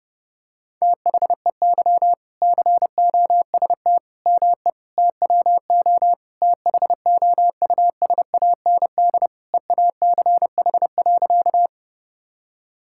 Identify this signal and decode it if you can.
Morse code, heard as a keyed tone, about 32 wpm, 700 Hz. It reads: THEY COST ME TWO THOUSAND EACH.
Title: Sentences from Top 1000 Words - Review Part 5 - 30wpm